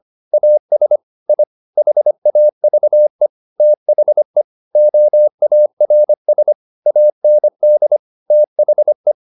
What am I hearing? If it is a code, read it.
Morse code, heard as a keyed tone, about 25 wpm, 600 Hz. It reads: AS I HAVE THE OARS AND THE